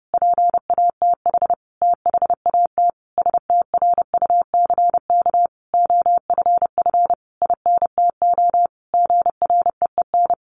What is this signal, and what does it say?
Morse code, heard as a keyed tone, about 30 wpm, 700 Hz. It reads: PATH THAT STRUCK OFF INTO GREEN
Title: A Sound Of Thunder 30wpm